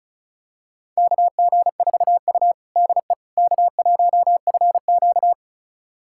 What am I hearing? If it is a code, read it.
Morse code, heard as a keyed tone, about 36 wpm, 700 Hz. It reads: KG4U DE K1FQ
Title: USA Callsign DE Callsign - 4 Characters - 35wpm